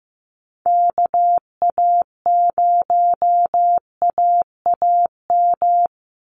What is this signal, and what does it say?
Morse code, heard as a keyed tone, about 15 wpm, 700 Hz. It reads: KA0AAM
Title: Call Signs Repeated 3x - 15wpm